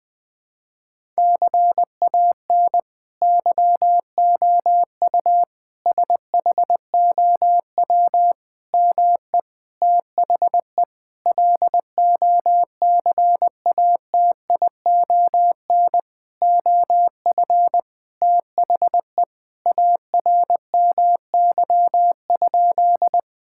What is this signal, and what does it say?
Morse code, heard as a keyed tone, about 20 wpm, 700 Hz. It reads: CAN YOU SHOW ME THE LOCATION OF THE ARMY?
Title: Sentences from Top 1000 Words - Review Part 6 - 20wpm